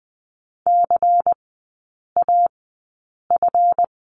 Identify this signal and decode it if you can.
Morse code, heard as a keyed tone, about 20 wpm, spaced as for 8 wpm, 700 Hz. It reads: CAF
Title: 178 Lesson 29e - Slash - Three Characters 20wpm